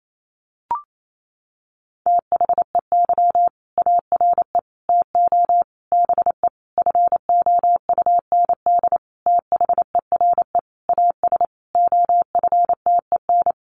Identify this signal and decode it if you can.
Morse code, heard as a keyed tone, about 28 wpm, 700 Hz. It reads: THEY ARE TO BE FOUND THERE AS OFTEN
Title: Sentences from Top 1400 Words 28wpm